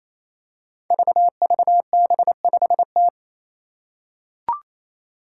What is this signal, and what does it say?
Morse code, heard as a keyed tone, about 28 wpm, 700 Hz. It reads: VVB5T
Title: Five Challenging Letter-Number Combinations 28wpm